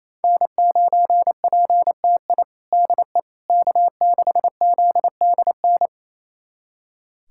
Morse code, heard as a keyed tone, about 28 wpm, 700 Hz. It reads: N9PTI DE K6ZDN
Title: USA Callsign DE Callsign - 5 Characters - 28wpm